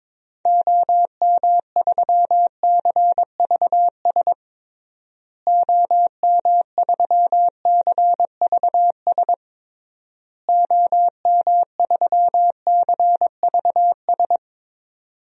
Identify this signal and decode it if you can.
Morse code, heard as a keyed tone, about 22 wpm, 700 Hz. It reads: OM3CVS OM3CVS OM3CVS
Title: DXCC-100 Call Signs - 6 Characters - Repeated 3x - 22wpm